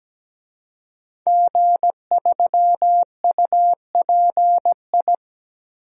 Morse code, heard as a keyed tone, about 17 wpm, 700 Hz. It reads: G3UPI
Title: DXCC-100 Call Signs - 5 Characters - 17wpm